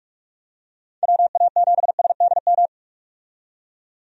Morse code, heard as a keyed tone, about 41 wpm, 700 Hz. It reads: WA7SDM
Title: USA Call Signs - 6 Characters - 45wpm